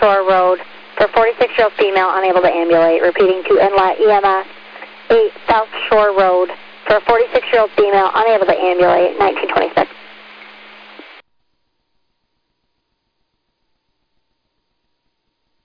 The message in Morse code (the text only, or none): none